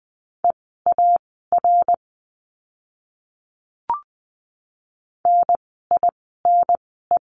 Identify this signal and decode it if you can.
Morse code, heard as a keyed tone, about 20 wpm, spaced as for 15 wpm, 700 Hz. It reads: EAR NINE
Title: 054 Lesson 10f - Letter r - Words 20wpm